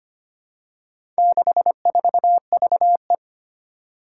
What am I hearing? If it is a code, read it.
Morse code, heard as a keyed tone, about 25 wpm, 700 Hz. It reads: 64VE
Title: Four Challenging Letter-Number Combinations 25wpm